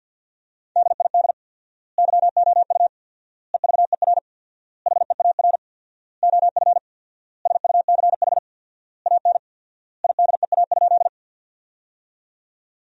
Morse code, heard as a keyed tone, about 53 wpm, 700 Hz. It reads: DID YOU EVER HEAR OF SUCH AN IDEA?
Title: 4x Word-Spacing - Sentences from Top 400 Words - 50wpm